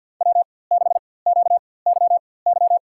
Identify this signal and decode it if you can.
Morse code, heard as a keyed tone, about 52 wpm, spaced as for 24 wpm, 700 Hz. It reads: W6QYY
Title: USA Call Signs - 5 Characters - 4x Character Spacing - 50wpm